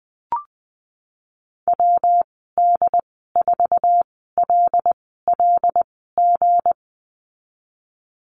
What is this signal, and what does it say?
Morse code, heard as a keyed tone, about 19 wpm, spaced as for 14 wpm, 700 Hz. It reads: WD4LLG
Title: 179 Lesson 29g - Slash - Call Signs 20wpm